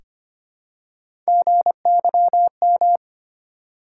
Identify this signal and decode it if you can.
Morse code, heard as a keyed tone, about 25 wpm, 700 Hz. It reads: GYM
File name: practice / 25wpm / Easy Words with Hints - 25wpm